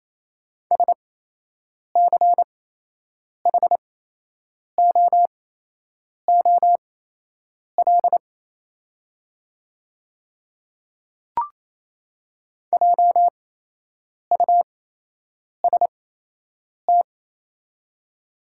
Morse code, heard as a keyed tone, about 28 wpm, spaced as for 8 wpm, 690 Hz. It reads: SCHOOL JUST
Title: 8x Character Spacing - Top 300 Words - 28wpm